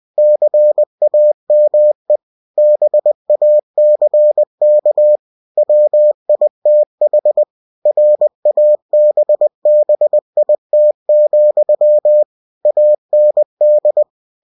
Morse code, heard as a keyed tone, about 20 wpm, 600 Hz. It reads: CAME BACK WITH RABBIT, AND
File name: cw-books / winnie-the-pooh / winnie-20wpm-page-0017